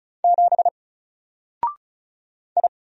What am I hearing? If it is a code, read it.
Morse code, heard as a keyed tone, about 36 wpm, 700 Hz. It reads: 7 I
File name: Single-Letter-Number-35wpm